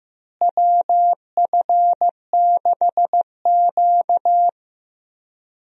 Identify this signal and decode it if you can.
Morse code, heard as a keyed tone, about 15 wpm, 700 Hz. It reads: WF6Q
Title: USA Call Signs - 4 Characters - Repeated 3x - 15wpm